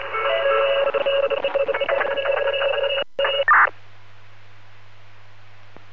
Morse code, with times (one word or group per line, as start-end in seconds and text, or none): none